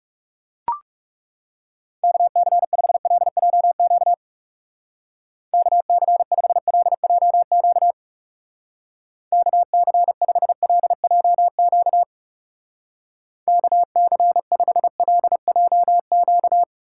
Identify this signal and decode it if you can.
Morse code, heard as a keyed tone, about 36 wpm, 700 Hz. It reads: KC5LJQ KC5LJQ KC5LJQ KC5LJQ
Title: Speed Racing - USA 2x3 Call Signs 30wpm